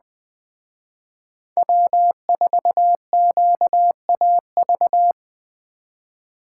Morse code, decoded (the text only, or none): W4QAV